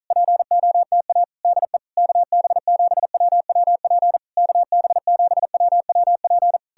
Morse code, heard as a keyed tone, about 46 wpm, 700 Hz. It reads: POTA DE KB7WWP KB7WWP